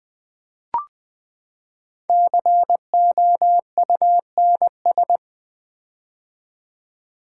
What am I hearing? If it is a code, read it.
Morse code, heard as a keyed tone, about 20 wpm, 700 Hz. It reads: COUNS
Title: Build A Word - 5 Letters At A Time - 10-Letter Words - 20wpm